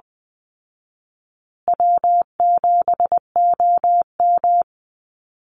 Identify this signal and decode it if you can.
Morse code, heard as a keyed tone, about 20 wpm, 700 Hz. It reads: W7OM
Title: Easy Call Signs Repeated 2x - 20wpm